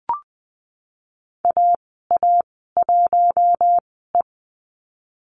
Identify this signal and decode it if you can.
Morse code, heard as a keyed tone, about 20 wpm, spaced as for 15 wpm, 700 Hz. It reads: AA1E
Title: 048 Lesson 9g - Number 4 - Call Signs 20wpm